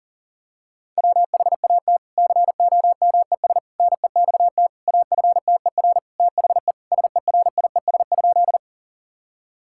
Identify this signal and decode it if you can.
Morse code, heard as a keyed tone, about 40 wpm, 700 Hz. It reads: WHAT COMES NEXT AFTER THE SERIES?
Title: Sentences from Top 1000 Words - Review Part 5 - 40wpm